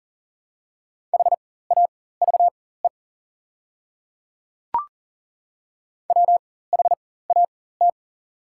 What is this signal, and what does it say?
Morse code, heard as a keyed tone, about 40 wpm, spaced as for 19 wpm, 700 Hz. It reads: HAVE WHAT
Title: 4x Character Spacing - Top 100 Words - 40wpm